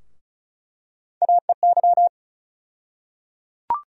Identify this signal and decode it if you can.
Morse code, heard as a keyed tone, about 34 wpm, 700 Hz. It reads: AEY
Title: USA Call Sign Postfixes - 3 Letters 35wpm